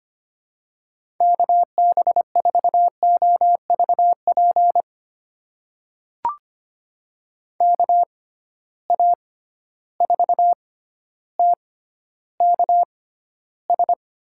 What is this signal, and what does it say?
Morse code, heard as a keyed tone, about 25 wpm, spaced as for 13 wpm, 700 Hz. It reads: KB4OVP KA4TKS
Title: USA Call Signs - 6 Characters - 6x Character Spacing - 25wpm